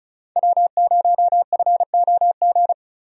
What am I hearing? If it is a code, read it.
Morse code, heard as a keyed tone, about 36 wpm, 700 Hz. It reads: W0FOG
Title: USA Call Signs - 5 Characters - 35wpm